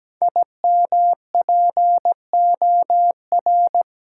Morse code, heard as a keyed tone, about 17 wpm, 700 Hz. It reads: IMPOR
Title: Sets of 3 Words - New Word Drill - 17wpm